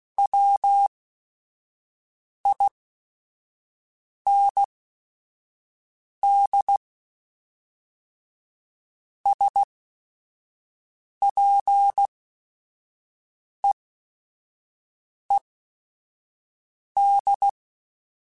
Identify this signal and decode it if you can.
Morse code, heard as a keyed tone, about 16 wpm, spaced as for 5 wpm, 800 Hz. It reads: WIND SPEED
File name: Weather Sets / Weather Sets-mp3s / Weather 5 1 @5